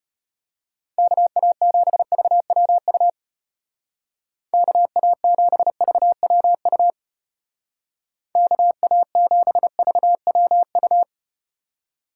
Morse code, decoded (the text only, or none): KA7VWU KA7VWU KA7VWU